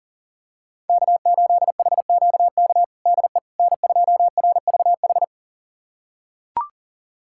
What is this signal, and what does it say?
Morse code, heard as a keyed tone, about 40 wpm, 700 Hz. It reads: K8HQK DE N2RVH
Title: USA Callsign DE Callsign - 5 Characters - 40wpm